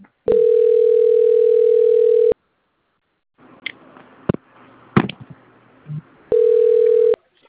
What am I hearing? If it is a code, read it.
Ringback tone (North America), 440 + 480 Hz.